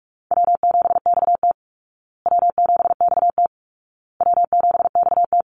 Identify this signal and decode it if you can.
Morse code, heard as a keyed tone, about 44 wpm, 700 Hz. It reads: W7XT W7XT W7XT
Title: Call Signs Repeated 2x - 45wpm